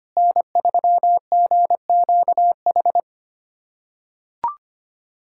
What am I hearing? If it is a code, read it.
Morse code, heard as a keyed tone, about 25 wpm, 700 Hz. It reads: N3GQH
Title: USA Callsign DE Callsign - 5 Characters - 25wpm